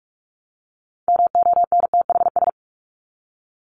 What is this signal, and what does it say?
Morse code, heard as a keyed tone, about 44 wpm, 700 Hz. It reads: MONTHS
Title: ES:S Suffix - Top 100 Words with Base Words 45wpm